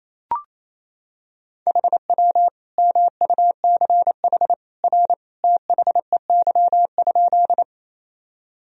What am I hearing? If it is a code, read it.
Morse code, heard as a keyed tone, about 28 wpm, 700 Hz. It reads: HW MUCH R THEY?
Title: Sentences from Top 200 Words and Abbreviations 28wpm